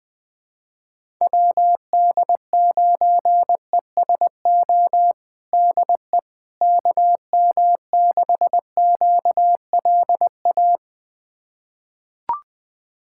Morse code, heard as a keyed tone, about 21 wpm, 700 Hz. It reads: WD9ESO DE KM6QLA
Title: USA Callsign DE Callsign 20wpm